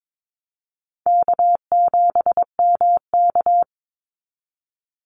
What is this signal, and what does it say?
Morse code, heard as a keyed tone, about 22 wpm, 700 Hz. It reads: K7MK
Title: NA SOTA Activator Callsigns 22wpm